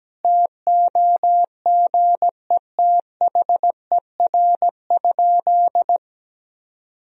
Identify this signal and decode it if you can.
Morse code, heard as a keyed tone, about 17 wpm, 700 Hz. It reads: TOGETHER?
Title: Sentences from Top 1000 Words - Review Part 4 - 17wpm